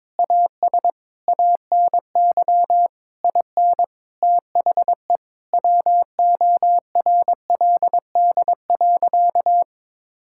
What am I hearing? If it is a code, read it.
Morse code, heard as a keyed tone, about 22 wpm, 700 Hz. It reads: AS ANY IN THE WORLD.
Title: Sentences from Top 1600 Words 22wpm